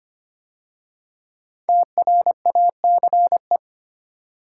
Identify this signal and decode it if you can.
Morse code, heard as a keyed tone, about 25 wpm, 700 Hz. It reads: TRACE
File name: 5 Letter Words - First 3 Letters Given - 25wpm